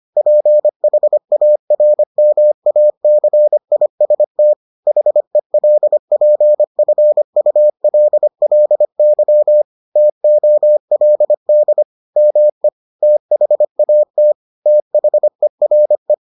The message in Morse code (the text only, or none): PHARMACIST HELPFULLY TOLD ME THAT THERE